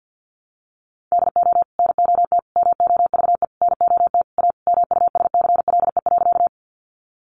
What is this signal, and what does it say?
Morse code, heard as a keyed tone, about 50 wpm, 700 Hz. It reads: DO NOT MOVE NOT A MUSCLE.
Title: Sentences from Top 1300 Words 50wpm